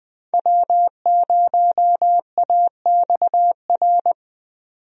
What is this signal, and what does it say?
Morse code, heard as a keyed tone, about 20 wpm, 700 Hz. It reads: W0AXR
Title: USA Call Signs - 5 Characters - 20wpm